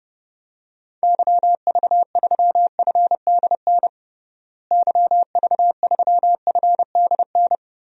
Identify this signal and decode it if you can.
Morse code, heard as a keyed tone, about 30 wpm, 700 Hz. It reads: YV3FDN YV3FDN
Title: DXCC-100 Call Signs - 6 Characters - Repeated 3x - 30wpm